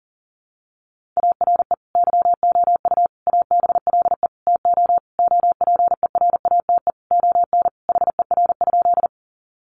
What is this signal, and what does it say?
Morse code, heard as a keyed tone, about 40 wpm, 700 Hz. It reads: ARE YOU ABLE TO OPERATE ON HER?
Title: Sentences from Top 1500 Words 40wpm